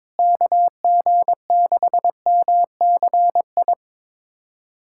Morse code, heard as a keyed tone, about 22 wpm, 700 Hz. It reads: KG6MCI
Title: USA Callsign DE Callsign 22wpm